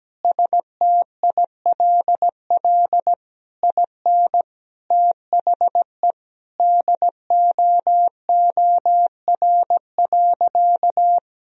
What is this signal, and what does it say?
Morse code, heard as a keyed tone, about 17 wpm, 700 Hz. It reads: STILL IN THE DOOR.
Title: Sentences from Top 1000 Words - Review Part 4 - 17wpm